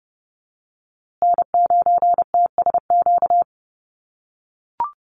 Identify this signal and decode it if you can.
Morse code, heard as a keyed tone, about 30 wpm, 700 Hz. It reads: N9TSQ
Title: USA Call Signs Repeated 2x 30wpm